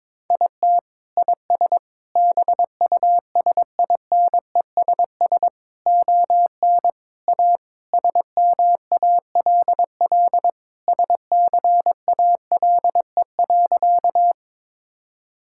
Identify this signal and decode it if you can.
Morse code, heard as a keyed tone, about 22 wpm, 700 Hz. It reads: IT IS BUSINESS ON A SMALL SCALE.
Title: Sentences from Top 1000 Words - Review Part 5 - 22wpm